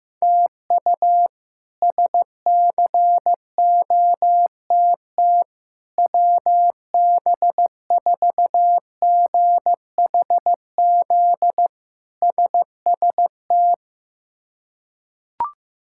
Morse code, heard as a keyed tone, about 15 wpm, 700 Hz. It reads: TU SCOTT WB4GHZ SST
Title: K1USN SST QSO Practice - 15wpm